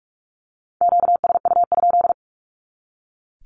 Morse code, heard as a keyed tone, about 44 wpm, 700 Hz. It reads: QSU?
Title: Q-Codes 45wpm